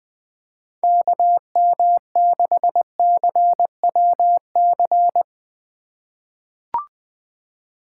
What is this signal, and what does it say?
Morse code, heard as a keyed tone, about 20 wpm, 700 Hz. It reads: KM6CWC
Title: USA Call Signs - 6 Characters - 2x Character Spacing - 20wpm